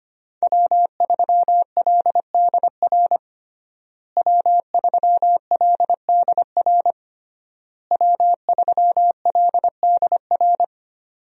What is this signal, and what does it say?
Morse code, heard as a keyed tone, about 25 wpm, 700 Hz. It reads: W3LDR W3LDR W3LDR
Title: USA Call Signs - 5 Characters - Repeated 3x - 25wpm